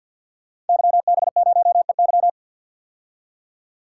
Morse code, heard as a keyed tone, about 51 wpm, 700 Hz. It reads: YB0EY
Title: Code Groups - 5 Letter:Numbers - 50wpm